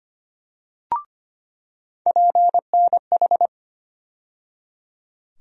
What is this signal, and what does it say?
Morse code, heard as a keyed tone, about 25 wpm, 700 Hz. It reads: PNH
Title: USA Call Sign Postfixes - 3 Letters 25wpm